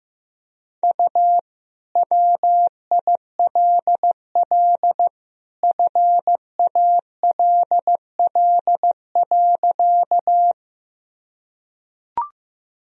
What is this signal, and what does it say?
Morse code, heard as a keyed tone, about 14 wpm, 700 Hz. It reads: U WILL FALL.
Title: Sentences from Top 500 Words and Abbreviations 15wpm